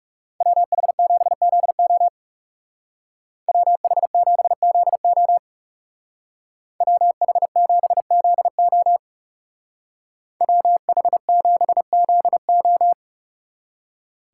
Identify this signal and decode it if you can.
Morse code, heard as a keyed tone, about 37 wpm, 700 Hz. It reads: WH7ZO WH7ZO WH7ZO WH7ZO
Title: Speed Racing - USA 2x2 Call Signs 30wpm